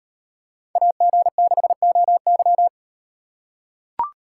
Morse code, heard as a keyed tone, about 37 wpm, 700 Hz. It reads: AG6OY